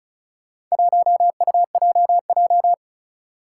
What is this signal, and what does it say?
Morse code, heard as a keyed tone, about 36 wpm, 700 Hz. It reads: 1UJJ
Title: Four Challenging Letter-Number Combinations 35wpm